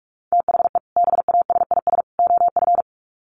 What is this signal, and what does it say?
Morse code, heard as a keyed tone, about 44 wpm, 700 Hz. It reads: THE BASIS OF
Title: Sets of 3 Words 45wpm